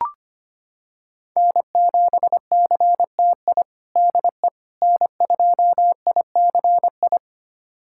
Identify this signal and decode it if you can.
Morse code, heard as a keyed tone, about 25 wpm, 700 Hz. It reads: N7CTI DE N2ICI
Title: USA Callsign DE Callsign - 5 Characters - 25wpm